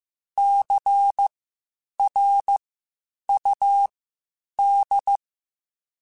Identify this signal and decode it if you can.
Morse code, heard as a keyed tone, about 15 wpm, spaced as for 8 wpm, 800 Hz. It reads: CRUD
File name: Spaced Alphas / Spaced Alphas mp3s / Cs sp @13